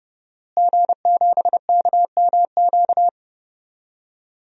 Morse code, heard as a keyed tone, about 31 wpm, 700 Hz. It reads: G7KMQ
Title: DXCC-100 Call Signs - 5 Characters - Repeated 2x - 30wpm